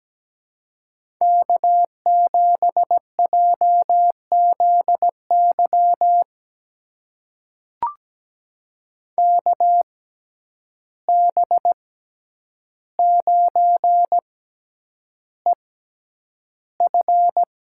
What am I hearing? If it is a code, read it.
Morse code, heard as a keyed tone, about 17 wpm, spaced as for 9 wpm, 700 Hz. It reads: K7JZY KB9EF
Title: USA Call Signs - 5 Characters - 6x Character Spacing - 17wpm